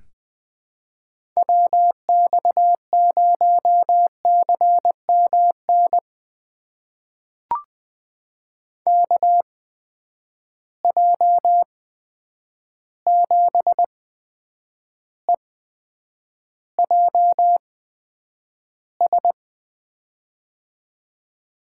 Morse code, heard as a keyed tone, about 20 wpm, spaced as for 9 wpm, 700 Hz. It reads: WX0CMN KJ7EJS
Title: USA Call Signs - 6 Characters - 8x Character Spacing - 20wpm